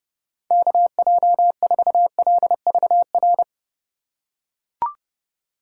Morse code, heard as a keyed tone, about 30 wpm, 700 Hz. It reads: KJ4LVR
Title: USA Call Signs - 6 Characters - 2x Character Spacing - 30wpm